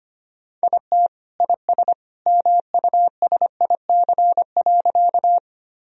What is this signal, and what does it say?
Morse code, heard as a keyed tone, about 25 wpm, 700 Hz. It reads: IT IS MUSIC.
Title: Sentences from Top 1000 Words - Review Part 1 - 25wpm